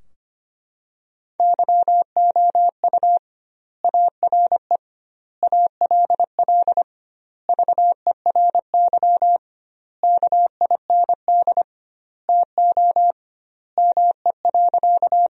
Morse code, heard as a keyed tone, about 25 wpm, 700 Hz. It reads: YOU ARE ALL VERY KIND TO ME.
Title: 2x Word-Spacing - Sentences from Top 200 Words - 25wpm